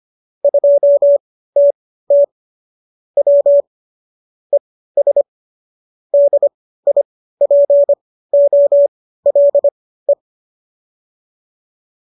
Morse code, heard as a keyed tone, about 25 wpm, spaced as for 15 wpm, 570 Hz. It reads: 2TT W ES DIPOLE